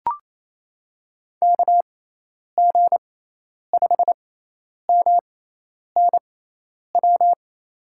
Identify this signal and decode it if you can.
Morse code, heard as a keyed tone, about 27 wpm, spaced as for 10 wpm, 700 Hz. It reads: KG5MNW
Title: USA Call Signs - 6 Characters - 6x Character Spacing - 28wpm